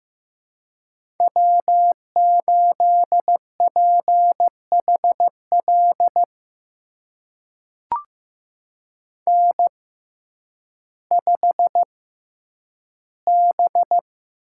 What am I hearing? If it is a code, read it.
Morse code, heard as a keyed tone, about 15 wpm, spaced as for 9 wpm, 700 Hz. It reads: W8PHL N5B
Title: USA Call Signs - 5 Characters - 6x Character Spacing - 15wpm